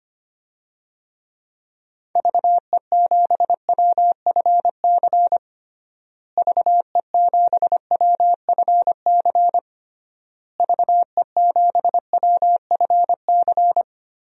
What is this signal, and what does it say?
Morse code, heard as a keyed tone, about 25 wpm, 700 Hz. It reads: VE7WFC VE7WFC VE7WFC
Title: DXCC-100 Call Signs - 6 Characters - Repeated 3x - 25wpm